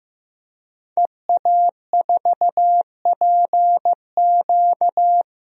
Morse code, heard as a keyed tone, about 16 wpm, 700 Hz. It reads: EA4PQ
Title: DXCC-100 Call Signs - 5 Characters - Repeated 2x - 15wpm